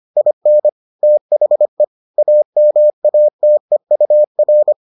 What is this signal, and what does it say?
Morse code, heard as a keyed tone, about 25 wpm, 600 Hz. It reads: IN THE AMATEUR